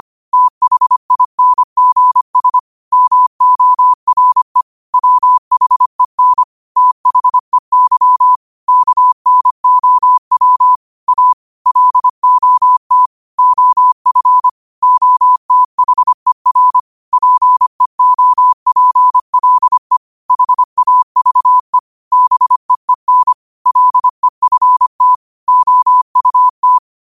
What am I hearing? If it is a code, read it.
Morse code, heard as a keyed tone, about 25 wpm, 1000 Hz. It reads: THINGS MORE WHEN THEY KNOW A LOT OF OTHER PEOPLE HAVE BEEN LEFT OUT
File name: qotdfiles / 25wpm